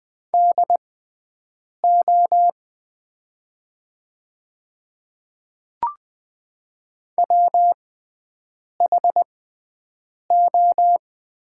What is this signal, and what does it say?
Morse code, heard as a keyed tone, about 20 wpm, spaced as for 7 wpm, 700 Hz. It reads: DO WHO
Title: 6x Character Spacing - Top 100 Words - 20wpm